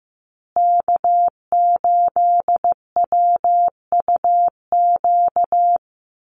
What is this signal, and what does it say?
Morse code, heard as a keyed tone, about 15 wpm, 700 Hz. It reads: K8WUQ
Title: USA Call Signs Repeated 3x 15wpm